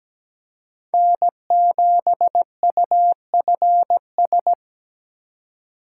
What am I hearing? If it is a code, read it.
Morse code, heard as a keyed tone, about 17 wpm, 700 Hz. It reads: N7UFS
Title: USA Call Signs - 5 Characters - 4x Character Spacing - 17wpm